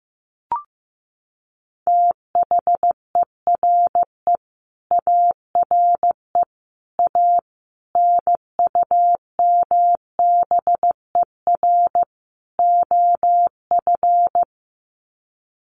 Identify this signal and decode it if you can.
Morse code, heard as a keyed tone, about 15 wpm, 700 Hz. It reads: THERE ARE A NUMBER OF
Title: Sets of 5 Words 15wpm